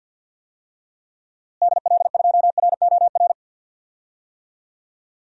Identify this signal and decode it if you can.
Morse code, heard as a keyed tone, about 51 wpm, 700 Hz. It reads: DL2ROR
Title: DXCC-100 Call Signs - 6 Characters - Repeated 2x - 50wpm